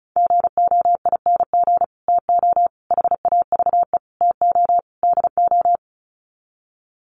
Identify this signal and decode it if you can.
Morse code, heard as a keyed tone, about 34 wpm, 700 Hz. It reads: GOING TO HAVE TO DO
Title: Sets of 5 Words Encore 35wpm